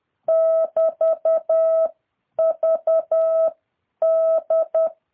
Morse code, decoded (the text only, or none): <BT> V D